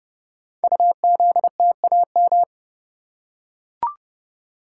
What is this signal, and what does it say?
Morse code, heard as a keyed tone, about 30 wpm, 700 Hz. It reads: UZTAM
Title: Code Groups - 5 Letters - 30wpm